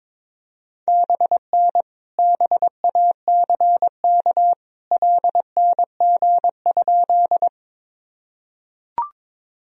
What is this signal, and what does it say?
Morse code, heard as a keyed tone, about 22 wpm, 700 Hz. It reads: BN BACK LNG?
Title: Sentences from Top 200 Words and Abbreviations 22wpm